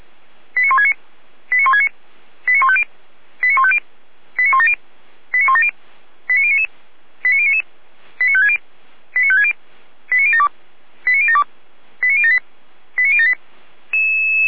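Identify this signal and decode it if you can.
Five-tone selective call (ZVEI), heard as quick runs of five tones, tones of about 70 ms. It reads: 89169 89169 89150 89150 89170 89170 89099 89099 89560 89560 89072 89072 89078 89078
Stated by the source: ZVEI five-tone selective call, code 89168, received on an SDR.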